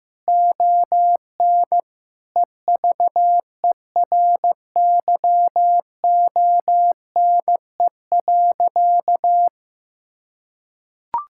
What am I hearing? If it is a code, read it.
Morse code, heard as a keyed tone, about 15 wpm, 700 Hz. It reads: ON EVERYONE.
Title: Sentences from Top 1000 Words - Review Part 6 - 15wpm